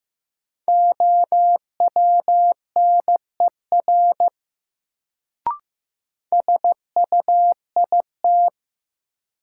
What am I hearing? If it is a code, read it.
Morse code, heard as a keyed tone, about 15 wpm, 700 Hz. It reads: OWNER SUIT
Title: Sentences from Top 1600 Words - New Word Drill - 15wpm